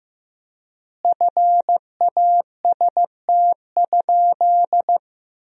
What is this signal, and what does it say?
Morse code, heard as a keyed tone, about 15 wpm, 700 Hz. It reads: FAST?
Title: 3x Word-Spacing - Sentences from Top 400 Words - 15wpm